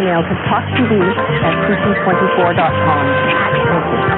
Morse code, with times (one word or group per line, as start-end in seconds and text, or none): none